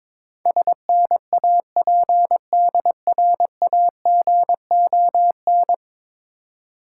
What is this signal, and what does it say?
Morse code, heard as a keyed tone, about 22 wpm, 690 Hz. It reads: SNAPDRAGON
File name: Difficult Words with Hints - 22wpm